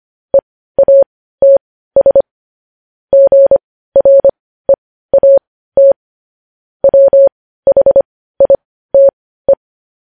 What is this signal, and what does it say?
Morse code, heard as a keyed tone, about 25 wpm, spaced as for 15 wpm, 550 Hz. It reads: EATS GREAT WHITE
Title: News Headlines at 15